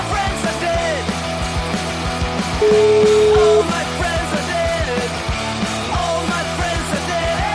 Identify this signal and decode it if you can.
Ringback tone (Europe, 425 Hz), with noise.